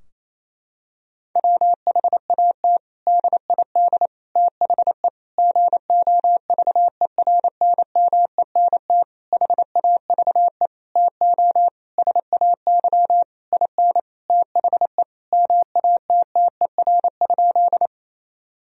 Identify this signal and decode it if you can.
Morse code, heard as a keyed tone, about 28 wpm, 700 Hz. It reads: WHAT DID THE GOVERNMENT HAVE TO SAY IN THE MATTER?
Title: Sentences from Top 1000 Words - Review Part 3 - 28wpm